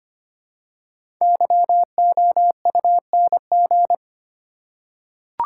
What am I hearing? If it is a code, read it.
Morse code, heard as a keyed tone, about 25 wpm, 700 Hz. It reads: YOUNG
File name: 5 Letter Words - First 3 Letters Given - 25wpm